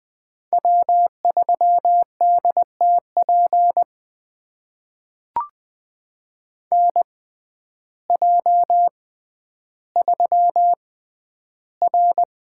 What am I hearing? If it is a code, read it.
Morse code, heard as a keyed tone, about 20 wpm, spaced as for 11 wpm, 700 Hz. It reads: W3DTP NJ3R
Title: USA Call Signs - 6x Character Spacing - 20wpm